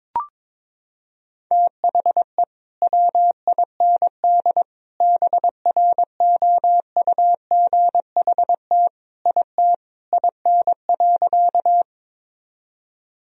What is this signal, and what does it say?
Morse code, heard as a keyed tone, about 22 wpm, 700 Hz. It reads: THE WIND BROUGHT IT IN.
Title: Sentences from Top 1000 Words - Review Part 2 - 22wpm